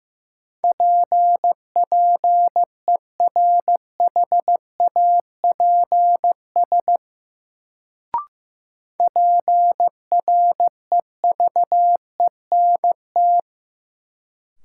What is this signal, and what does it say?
Morse code, heard as a keyed tone, about 16 wpm, 700 Hz. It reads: PPERHAPS PREVENT
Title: Sentences from Top 1100 Words - New Word Drill - 15wpm